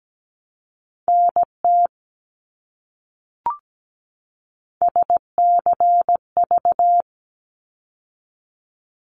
Morse code, heard as a keyed tone, about 17 wpm, 700 Hz. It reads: NT SCV
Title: ARRL Sections 17wpm